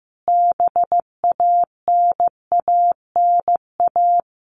Morse code, heard as a keyed tone, about 15 wpm, 700 Hz. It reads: BANANA
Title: Easy Phrases 15wpm